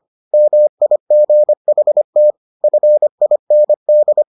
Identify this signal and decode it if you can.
Morse code, heard as a keyed tone, about 25 wpm, 600 Hz. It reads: MIGHT FIND